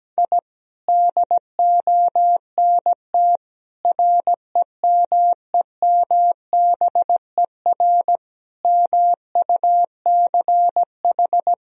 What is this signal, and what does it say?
Morse code, heard as a keyed tone, about 17 wpm, 700 Hz. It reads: I DONT REMEMBER MUCH
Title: Sentences from Top 1000 Words - Review Part 1 - 17wpm